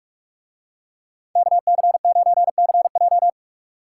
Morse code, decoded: KC9CJ

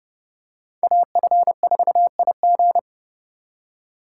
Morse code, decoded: AF4IG

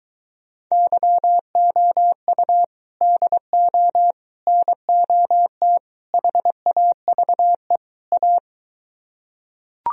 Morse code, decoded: YOU DO NOT HAVE A

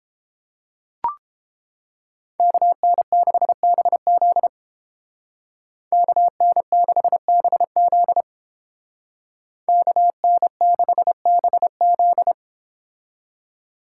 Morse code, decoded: KN6BZ KN6BZ KN6BZ

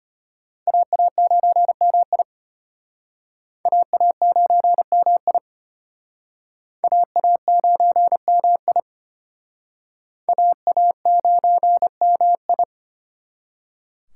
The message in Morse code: AA9MI AA9MI AA9MI AA9MI

